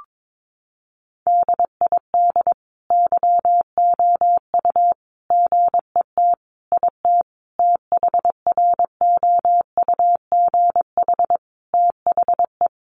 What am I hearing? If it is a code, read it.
Morse code, heard as a keyed tone, about 22 wpm, 700 Hz. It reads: DID YOU GET IT THROUGH THE
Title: Sentences from Top 1300 Words 22wpm